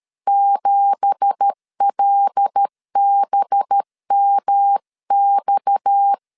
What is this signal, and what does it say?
Morse code, heard as a keyed tone, about 13 wpm, 800 Hz. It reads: 7LBMX
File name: CW / DET / 1 9 1500C S B 12wpm 3819